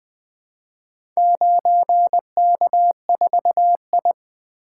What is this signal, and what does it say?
Morse code, heard as a keyed tone, about 20 wpm, 700 Hz. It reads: 9K4I
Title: Four Letter-Number Combinations - 20wpm